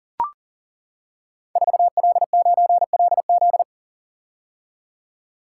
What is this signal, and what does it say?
Morse code, heard as a keyed tone, about 38 wpm, 700 Hz. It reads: 4L9LZ
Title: Code Groups - 5 Letter:Numbers - 40wpm